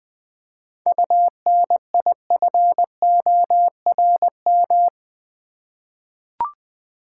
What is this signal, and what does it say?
Morse code, heard as a keyed tone, about 20 wpm, 700 Hz. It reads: UNIFORM
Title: 7 Letter Words - First 3 Letters Given - 20wpm